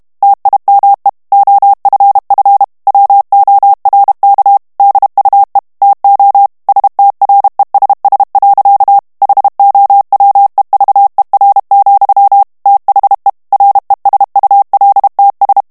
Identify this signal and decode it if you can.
Morse code, heard as a keyed tone, about 32 wpm, 800 Hz. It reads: TIME OFF WORK DUE TO STRESS. HOWEVER, THE RESULTS
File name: morse mp3 / ANGLAIS texte clair 1 vitesse 30 mots minute